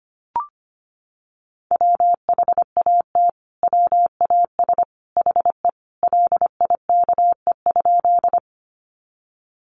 Morse code, heard as a keyed tone, about 25 wpm, 700 Hz. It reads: WHAT WAS HE LIKE?
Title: Sentences from Top 100 Words 25wpm